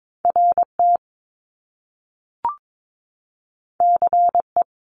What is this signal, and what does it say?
Morse code, heard as a keyed tone, about 22 wpm, 700 Hz. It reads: RT CE
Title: Top Two Letter Combinations in Words 22wpm